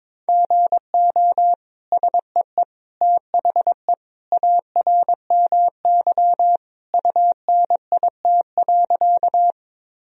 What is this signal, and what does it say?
Morse code, heard as a keyed tone, about 22 wpm, 700 Hz. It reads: GO SEE THE ARMY UNIT.
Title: Sentences from Top 1000 Words - Review Part 2 - 22wpm